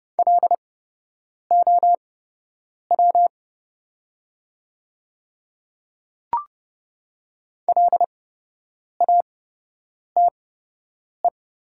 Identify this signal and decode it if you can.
Morse code, heard as a keyed tone, about 30 wpm, spaced as for 8 wpm, 700 Hz. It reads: LOW LATE
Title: 8x Character Spacing - Top 500 Words - 30wpm